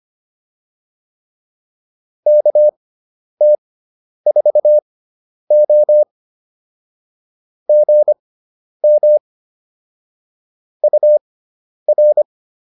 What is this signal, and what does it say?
Morse code, heard as a keyed tone, about 25 wpm, spaced as for 10 wpm, 600 Hz. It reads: KT4O GM UR